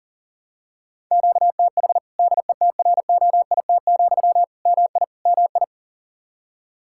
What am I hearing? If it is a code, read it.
Morse code, heard as a keyed tone, about 40 wpm, 700 Hz. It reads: QTH DETROIT, MI MI
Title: Practice QSO - Exchange 1 - Beginning - 40wpm